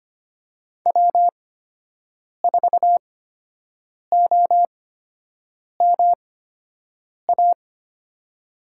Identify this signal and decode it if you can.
Morse code, heard as a keyed tone, about 25 wpm, spaced as for 7 wpm, 700 Hz. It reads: W4OMA
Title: USA Call Signs - 5 Characters - 8x Character Spacing - 25wpm